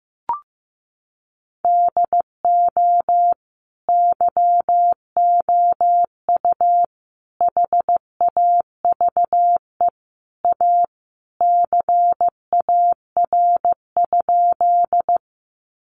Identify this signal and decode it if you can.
Morse code, heard as a keyed tone, about 14 wpm, 700 Hz. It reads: DO YOU HAVE A CAR?
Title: Sentences from Top 300 Words 15wpm